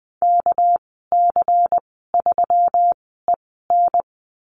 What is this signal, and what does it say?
Morse code, heard as a keyed tone, about 20 wpm, spaced as for 15 wpm, 700 Hz. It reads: KC3EN
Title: 200 Lesson 32g - Letter k - Call Signs 20wpm